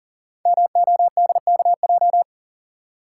Morse code, heard as a keyed tone, about 39 wpm, 700 Hz. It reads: MODKJ
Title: Code Groups - 5 Letter:Numbers - 40wpm